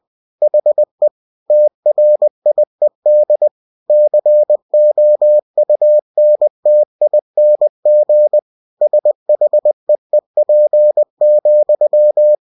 Morse code, heard as a keyed tone, about 20 wpm, 600 Hz. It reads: HE TRIED COUNTING SHEEP,
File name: cw-books / winnie-the-pooh / winnie-20wpm-page-0039